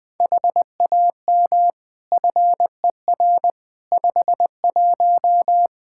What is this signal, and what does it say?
Morse code, heard as a keyed tone, about 20 wpm, 700 Hz. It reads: HAM FER 51
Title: Practice QSO - Exchange 3 - More Info - 20wpm